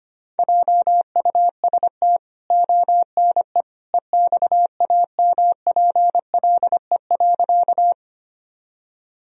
Morse code, heard as a keyed tone, about 25 wpm, 700 Hz. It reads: JUST ONE EXAMPLE.
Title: Sentences from Top 1000 Words - Review Part 1 - 25wpm